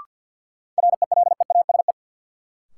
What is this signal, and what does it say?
Morse code, heard as a keyed tone, about 50 wpm, 700 Hz. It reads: RELEASE
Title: Sentences from Top 1100 Words - New Word Drill - 50wpm